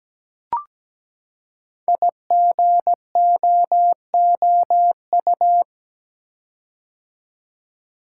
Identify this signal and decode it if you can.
Morse code, heard as a keyed tone, about 17 wpm, 700 Hz. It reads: IGOOU
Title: Code Groups - 5 Letters - 17wpm